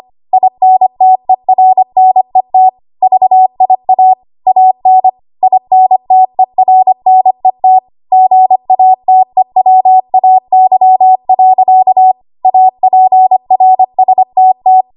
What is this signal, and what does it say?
Morse code, heard as a keyed tone, about 25 wpm, 750 Hz. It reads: INTERNET VIA AN INTERNET GATEWAY. APRSTT
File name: Morse / 25WPM